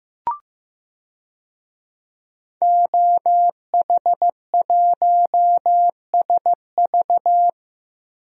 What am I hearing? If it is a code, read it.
Morse code, heard as a keyed tone, about 15 wpm, 700 Hz. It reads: OH1SV
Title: DXCC-100 Call Signs - 5 Characters - Repeated 2x - 15wpm